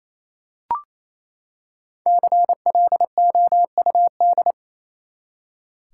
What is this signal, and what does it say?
Morse code, heard as a keyed tone, about 28 wpm, 700 Hz. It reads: CLOUD
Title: Five Letter Words - 28wpm